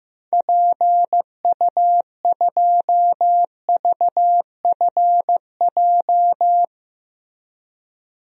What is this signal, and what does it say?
Morse code, heard as a keyed tone, about 16 wpm, 700 Hz. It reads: PU2VFJ
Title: DXCC-100 Call Signs - 6 Characters - Repeated 2x - 15wpm